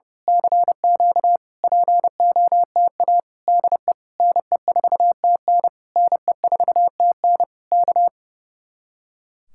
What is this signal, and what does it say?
Morse code, heard as a keyed tone, about 28 wpm, 700 Hz. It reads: CQ POTA DE NE4TN NE4TN K
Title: Speed-Racing Common QSO Phrases - 30wpm